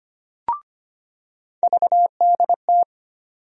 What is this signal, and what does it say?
Morse code, heard as a keyed tone, about 25 wpm, 700 Hz. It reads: VDT
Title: USA Call Sign Postfixes - 3 Letters 25wpm